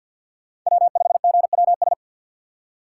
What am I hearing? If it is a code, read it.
Morse code, heard as a keyed tone, about 51 wpm, 700 Hz. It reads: W5GWS
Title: USA Call Signs - 5 Characters - 6x Character Spacing - 50wpm